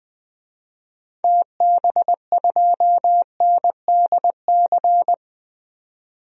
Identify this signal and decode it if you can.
Morse code, heard as a keyed tone, about 20 wpm, 700 Hz. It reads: TB2NDC
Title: DXCC-100 Call Signs - 6 Characters - 20wpm